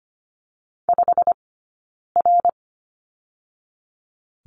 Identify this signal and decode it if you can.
Morse code, heard as a keyed tone, about 25 wpm, 700 Hz. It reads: 5 R
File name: Lesson 15d - Number 5 - Two Characters 25wpm